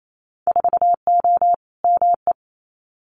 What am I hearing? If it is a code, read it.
Morse code, heard as a keyed tone, about 28 wpm, 700 Hz. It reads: 4O ME
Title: Winter Field Day Exchanges Encore 28wpm